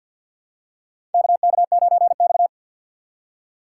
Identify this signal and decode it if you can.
Morse code, heard as a keyed tone, about 53 wpm, 700 Hz. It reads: KK9X
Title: USA Call Signs - 4 Characters - 3x Character Spacing - 50wpm